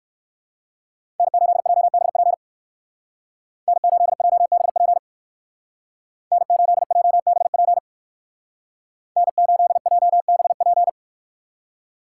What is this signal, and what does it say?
Morse code, heard as a keyed tone, about 58 wpm, 700 Hz. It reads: N8JBP N8JBP N8JBP N8JBP